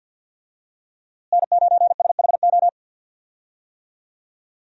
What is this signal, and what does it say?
Morse code, heard as a keyed tone, about 50 wpm, 700 Hz. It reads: N9SHO
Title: USA Call Signs - 5 Characters - Repeated 2x - 50wpm